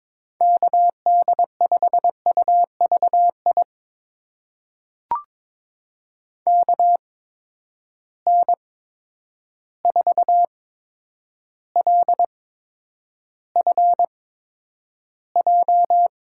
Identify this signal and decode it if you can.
Morse code, heard as a keyed tone, about 22 wpm, spaced as for 9 wpm, 700 Hz. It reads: KD5UVI KN4LFJ
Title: USA Call Signs - 6 Characters - 8x Character Spacing - 22wpm